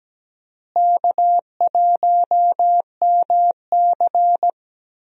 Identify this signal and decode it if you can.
Morse code, heard as a keyed tone, about 17 wpm, 700 Hz. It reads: K1MC
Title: USA Call Signs - 4 Characters - Repeated 3x - 17wpm